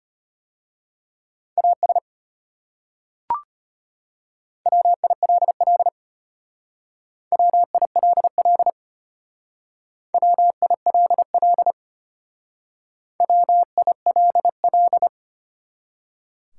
Morse code, heard as a keyed tone, about 34 wpm, 700 Hz. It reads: AS WILL WILL WILL WILL